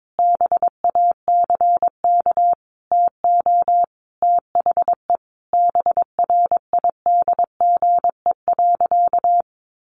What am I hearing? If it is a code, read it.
Morse code, heard as a keyed tone, about 22 wpm, 700 Hz. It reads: BACK TO THE BRIDGE.
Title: Sentences from Top 1200 Words 22wpm